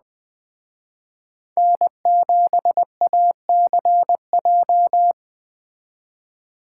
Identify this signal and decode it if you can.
Morse code, heard as a keyed tone, about 20 wpm, 700 Hz. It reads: N7ACJ
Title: Speed Racing - USA Call Signs 17wpm